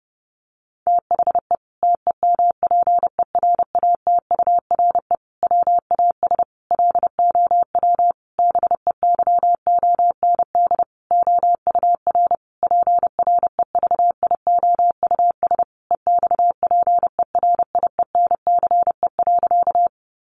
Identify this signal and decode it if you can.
Morse code, heard as a keyed tone, about 28 wpm, 700 Hz. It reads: THE TEMPERATURE WAS LOW BEYOND OUR PREVIOUS EXPERIENCE.
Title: Sentences from Top 1500 Words 30wpm